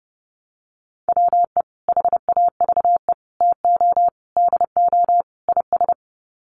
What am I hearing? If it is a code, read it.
Morse code, heard as a keyed tone, about 30 wpm, 700 Hz. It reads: WE HAVE TO DO IS
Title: Sets of 5 Words Encore 30wpm